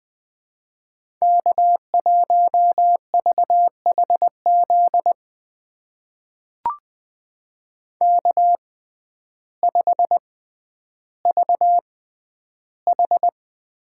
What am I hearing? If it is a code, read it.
Morse code, heard as a keyed tone, about 20 wpm, spaced as for 11 wpm, 700 Hz. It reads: K1VHZ K5VH
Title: USA Call Signs - 5 Characters - 6x Character Spacing - 20wpm